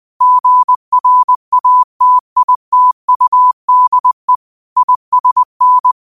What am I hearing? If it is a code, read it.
Morse code, heard as a keyed tone, about 20 wpm, 1000 Hz. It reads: GRATITUDE ISN